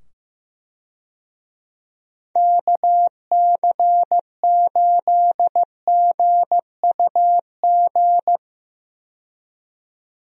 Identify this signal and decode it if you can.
Morse code, heard as a keyed tone, about 15 wpm, 700 Hz. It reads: KC8GUG
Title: USA Call Signs - 6 Characters - 15wpm